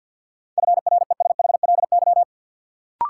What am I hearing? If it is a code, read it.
Morse code, heard as a keyed tone, about 46 wpm, 700 Hz. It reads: FRESHLY
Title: Sets of 5 Words Encore - New Word Drill - 50wpm